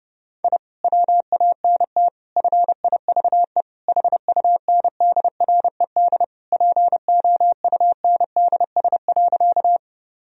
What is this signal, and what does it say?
Morse code, heard as a keyed tone, about 30 wpm, 700 Hz. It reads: I WANT FIVE HUNDRED POUNDS.
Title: Sentences from Top 1000 Words - Review Part 2 - 30wpm